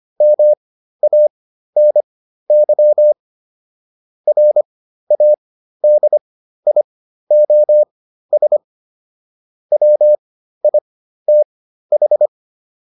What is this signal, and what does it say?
Morse code, heard as a keyed tone, about 25 wpm, spaced as for 13 wpm, 600 Hz. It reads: MANY RADIOS WITH